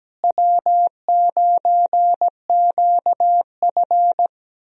Morse code, heard as a keyed tone, about 17 wpm, 700 Hz. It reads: W9QF
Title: USA Call Signs - 4 Characters - Repeated 3x - 17wpm